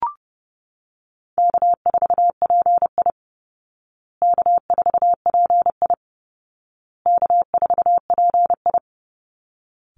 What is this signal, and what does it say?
Morse code, heard as a keyed tone, about 30 wpm, 700 Hz. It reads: K4PI K4PI K4PI
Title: Easy Call Signs Repeated 3x - 30wpm